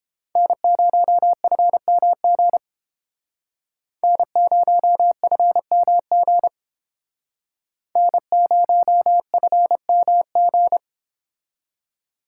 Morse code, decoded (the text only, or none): N0FMG N0FMG N0FMG